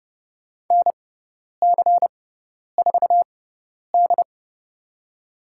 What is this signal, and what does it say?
Morse code, heard as a keyed tone, about 32 wpm, spaced as for 10 wpm, 700 Hz. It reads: NC4D
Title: USA Call Signs - 4 Characters - 6x Character Spacing - 30wpm